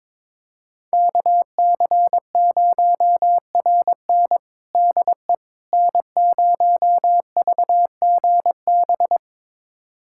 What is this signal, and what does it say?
Morse code, heard as a keyed tone, about 21 wpm, 700 Hz. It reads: KC0RN DE N0VGB
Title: USA Callsign DE Callsign 22wpm